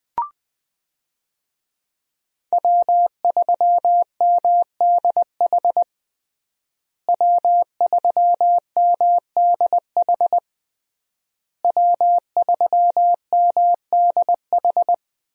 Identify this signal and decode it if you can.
Morse code, heard as a keyed tone, about 21 wpm, 700 Hz. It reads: W3MDH W3MDH W3MDH
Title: USA Call Signs - 5 Characters - Repeated 3x - 20wpm